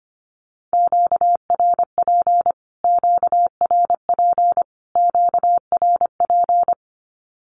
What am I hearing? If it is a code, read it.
Morse code, heard as a keyed tone, about 25 wpm, 700 Hz. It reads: QRP QRP QRP
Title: Q-Codes Repeated 3x 25wpm